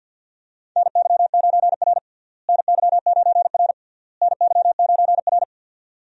Morse code, heard as a keyed tone, about 47 wpm, 700 Hz. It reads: NY9R NY9R NY9R
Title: USA Call Signs - 4 Characters - Repeated 3x - 50wpm